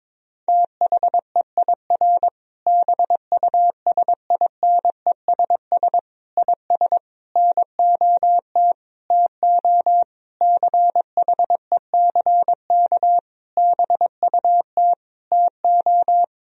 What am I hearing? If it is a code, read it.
Morse code, heard as a keyed tone, about 22 wpm, 700 Hz. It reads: THEIR BUSINESS IS NOT TO CHECK BUT TO
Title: Sentences from Top 1000 Words - Review Part 3 - 22wpm